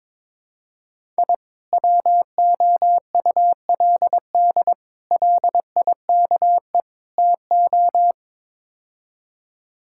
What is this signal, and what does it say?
Morse code, heard as a keyed tone, about 21 wpm, 700 Hz. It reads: I WOULD LIKE TO
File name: practice / 22wpm / Build A Phrase - 5-Word Phrases - 22wpm